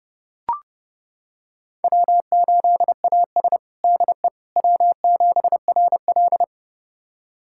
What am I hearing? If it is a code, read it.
Morse code, heard as a keyed tone, about 30 wpm, 700 Hz. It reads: W8AS DE W7RL
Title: USA Callsign DE Callsign - 4 Characters - 30wpm